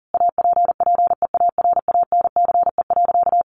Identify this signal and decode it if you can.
Morse code, heard as a keyed tone, about 40 wpm, 700 Hz. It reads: APPEARANCE.
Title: Sentences from Top 1100 Words 40wpm